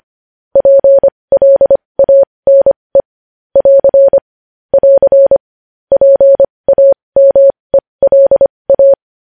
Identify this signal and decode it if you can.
Morse code, heard as a keyed tone, about 25 wpm, spaced as for 20 wpm, 550 Hz. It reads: PLANE <AR> <AR> PAMELA
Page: News Headlines in Morse Code at 20 WPM